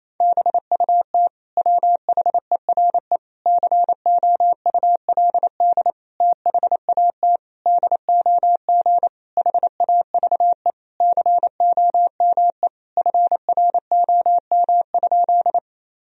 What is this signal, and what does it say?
Morse code, heard as a keyed tone, about 27 wpm, 700 Hz. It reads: BUT WHERE COULD THAT DOG HAVE COME FROM?
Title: Sentences from Top 1000 Words - Review Part 2 - 28wpm